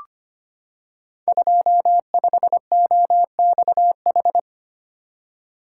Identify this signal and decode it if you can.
Morse code, heard as a keyed tone, about 25 wpm, 700 Hz. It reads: 25OXH
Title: Code Groups - 5 Letter:Numbers - 25wpm